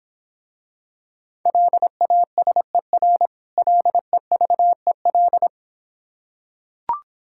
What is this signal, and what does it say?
Morse code, heard as a keyed tone, about 26 wpm, 690 Hz. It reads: LASER LEVEL